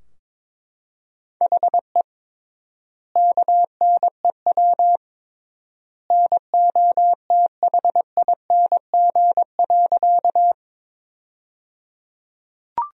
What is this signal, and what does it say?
Morse code, heard as a keyed tone, about 22 wpm, 700 Hz. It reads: HE KNEW NOTHING.
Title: 3x Word-Spacing - Sentences from Top 500 Words - 22wpm